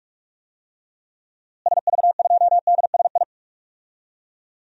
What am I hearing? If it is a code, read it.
Morse code, heard as a keyed tone, about 46 wpm, 700 Hz. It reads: SV2DSI
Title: DXCC-100 Call Signs - 6 Characters - Repeated 3x - 45wpm